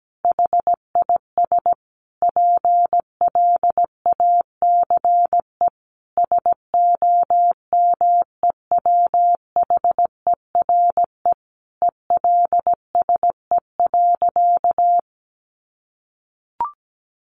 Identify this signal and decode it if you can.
Morse code, heard as a keyed tone, about 17 wpm, 700 Hz. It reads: HIS PLACE SOMEWHERE ELSE.
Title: Sentences from Top 1400 Words 17wpm